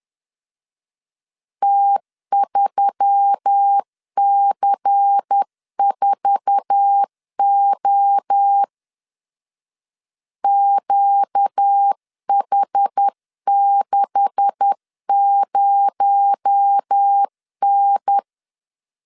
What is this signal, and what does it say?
Morse code, heard as a keyed tone, about 10 wpm, 800 Hz. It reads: T3C4O QH60N
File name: LEZIONI / 9 286 10wpm 2735